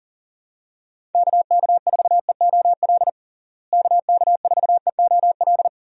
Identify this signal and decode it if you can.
Morse code, heard as a keyed tone, about 40 wpm, 700 Hz. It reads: KK4EOL KK4EOL
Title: USA Call Signs - 6 Characters - Repeated 3x - 40wpm